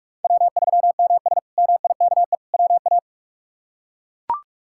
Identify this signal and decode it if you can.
Morse code, heard as a keyed tone, about 42 wpm, 700 Hz. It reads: W3MS MIKE WA
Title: K1USN SST QSO Practice - 45wpm